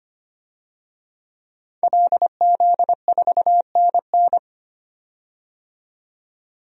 Morse code, heard as a keyed tone, about 25 wpm, 690 Hz. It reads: LZ4NN
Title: DXCC-100 Call Signs - 5 Characters - 25wpm